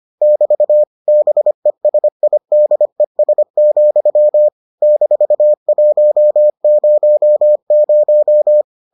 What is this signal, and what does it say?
Morse code, heard as a keyed tone, about 25 wpm, 600 Hz. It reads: <BT> BESIDES, -100